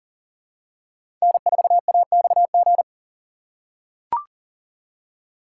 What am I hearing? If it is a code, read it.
Morse code, heard as a keyed tone, about 40 wpm, 700 Hz. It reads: N4AXG
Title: USA Call Signs - 5 Characters - 3x Character Spacing - 40wpm